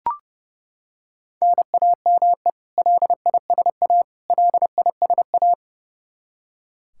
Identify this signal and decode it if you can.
Morse code, heard as a keyed tone, about 32 wpm, 700 Hz. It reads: NAME LISA LISA
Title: Practice QSO - Exchange 1 - Beginning - 30wpm